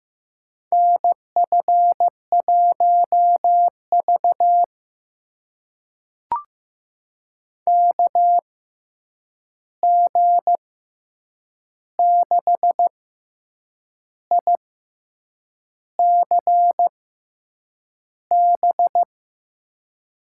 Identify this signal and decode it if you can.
Morse code, heard as a keyed tone, about 15 wpm, spaced as for 7 wpm, 700 Hz. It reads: NF1V KG6ICB
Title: USA Call Signs - 6x Character Spacing - 15wpm